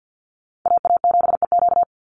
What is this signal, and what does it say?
Morse code, heard as a keyed tone, about 49 wpm, 700 Hz. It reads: AA7EQ
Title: NA SOTA Chaser Callsigns 50wpm